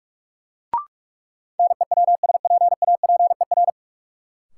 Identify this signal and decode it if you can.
Morse code, heard as a keyed tone, about 41 wpm, 700 Hz. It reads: NEWSPAPER
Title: Sentences from Top 1000 Words - Review Part 4 - New Word Drill - 45wpm